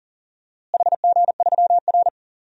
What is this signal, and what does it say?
Morse code, heard as a keyed tone, about 37 wpm, 700 Hz. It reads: HG3R
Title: DXCC-100 Call Signs - 4 Characters - Repeated 2x - 40wpm